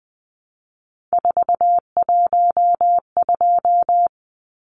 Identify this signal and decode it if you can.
Morse code, heard as a keyed tone, about 20 wpm, 700 Hz. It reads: 412
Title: Numbers - 3-Digit 20wpm